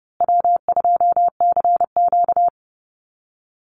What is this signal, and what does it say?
Morse code, heard as a keyed tone, about 30 wpm, 700 Hz. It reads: W2CQ
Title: Easy Call Signs Repeated 3x - 30wpm